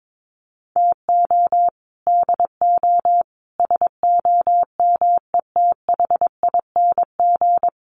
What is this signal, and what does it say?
Morse code, heard as a keyed tone, about 22 wpm, 700 Hz. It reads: TO DO SOMETHING
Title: Sets of 5 Words 22wpm